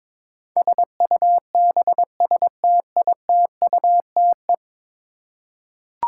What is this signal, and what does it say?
Morse code, heard as a keyed tone, about 22 wpm, 700 Hz. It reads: SUBSTITUTE
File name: practice / 22wpm / 10 Letter Words - First 3 Letters Given - 22wpm